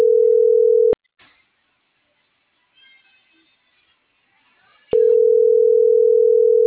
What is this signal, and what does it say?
Ringback tone (North America), 440 + 480 Hz.